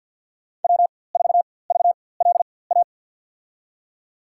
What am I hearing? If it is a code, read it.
Morse code, heard as a keyed tone, about 52 wpm, spaced as for 24 wpm, 700 Hz. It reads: W4VLA
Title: USA Call Signs - 5 Characters - 4x Character Spacing - 50wpm